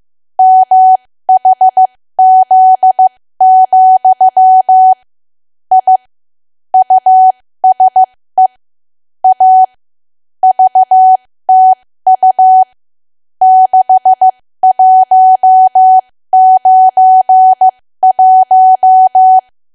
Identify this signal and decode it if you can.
Morse code, heard as a keyed tone, about 15 wpm, spaced as for 13 wpm, 750 Hz. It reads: MHZ, I USE A VTU 6191
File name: cwa / W1AW-Level2 / 13WPM